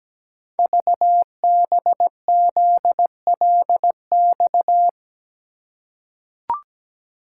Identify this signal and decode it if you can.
Morse code, heard as a keyed tone, about 17 wpm, 700 Hz. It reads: VBZLX
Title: Code Groups - 5 Letters - 17wpm